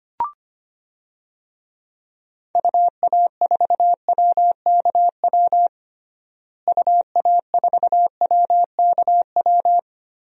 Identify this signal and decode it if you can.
Morse code, heard as a keyed tone, about 25 wpm, 700 Hz. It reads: UA4WKW UA4WKW
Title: DXCC-100 Call Signs - 6 Characters - Repeated 3x - 25wpm